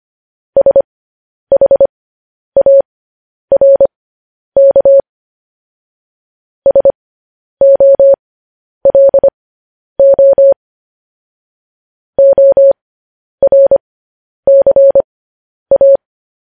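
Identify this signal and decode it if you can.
Morse code, heard as a keyed tone, about 25 wpm, spaced as for 10 wpm, 550 Hz. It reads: SHARK SOLO ORCA
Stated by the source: Morse code transcription of news headlines sent at 10 wpm.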